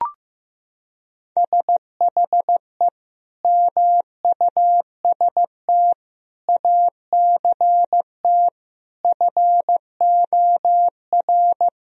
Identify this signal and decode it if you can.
Morse code, heard as a keyed tone, about 15 wpm, 700 Hz. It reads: SHE MUST ACT FOR
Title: Sentences from Top 1000 Words - Review Part 1 - 15wpm